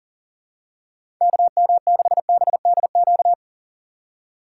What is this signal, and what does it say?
Morse code, heard as a keyed tone, about 40 wpm, 700 Hz. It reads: KM6BDQ
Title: USA Call Signs - 6 Characters - 3x Character Spacing - 40wpm